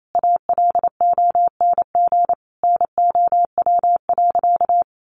Morse code, heard as a keyed tone, about 28 wpm, 700 Hz. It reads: ALONG NOW.
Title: Sentences from Top 700 Words 28wpm